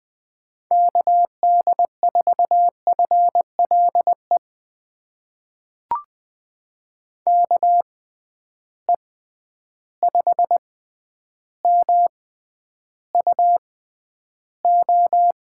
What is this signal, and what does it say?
Morse code, heard as a keyed tone, about 20 wpm, spaced as for 10 wpm, 700 Hz. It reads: KD4FLE KE5MUO
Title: USA Call Signs - 6 Characters - 6x Character Spacing - 20wpm